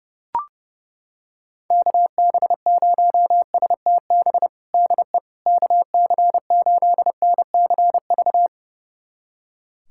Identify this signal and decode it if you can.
Morse code, heard as a keyed tone, about 30 wpm, 700 Hz. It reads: KB0STB DE KC8NCV
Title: USA Callsign DE Callsign - 6 Characters - 30wpm